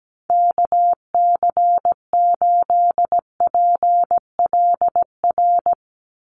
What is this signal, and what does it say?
Morse code, heard as a keyed tone, about 17 wpm, 700 Hz. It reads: KC8PLR
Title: USA Call Signs 17wpm